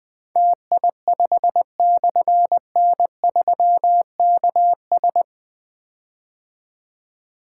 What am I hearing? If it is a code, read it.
Morse code, heard as a keyed tone, about 20 wpm, 700 Hz. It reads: TI5/N3KS
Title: Difficult Call Signs 20wpm